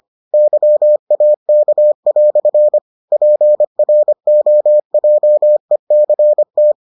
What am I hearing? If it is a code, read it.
Morse code, heard as a keyed tone, about 25 wpm, 600 Hz. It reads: YAK" PROJECT